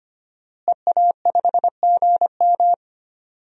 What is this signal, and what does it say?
Morse code, heard as a keyed tone, about 25 wpm, 700 Hz. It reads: EA5GM
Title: DXCC-100 Call Signs - 5 Characters - Repeated 2x - 25wpm